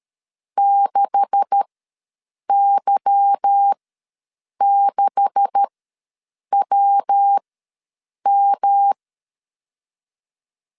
Morse code, heard as a keyed tone, about 13 wpm, spaced as for 7 wpm, 800 Hz. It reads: 6Y6WM
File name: LEZIONI / 9 286 ws 12wpm 3351